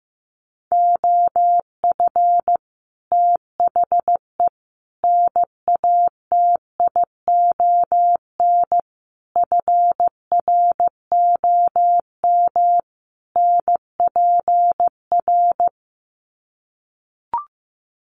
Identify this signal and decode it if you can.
Morse code, heard as a keyed tone, about 15 wpm, 700 Hz. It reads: OF THE NATION FROM NPR
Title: Sets of 5 Words 15wpm